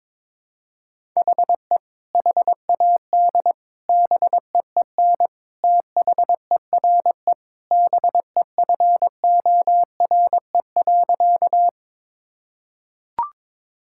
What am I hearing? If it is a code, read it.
Morse code, heard as a keyed tone, about 22 wpm, 700 Hz. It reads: HE HAD BEEN THERE BEFORE.
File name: practice / 22wpm / Sentences with 3-Word Phrases - 22wpm